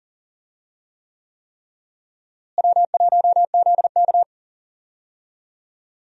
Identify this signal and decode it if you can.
Morse code, heard as a keyed tone, about 40 wpm, 700 Hz. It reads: W1ZK
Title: USA Call Signs - 4 Characters - Repeated 3x - 40wpm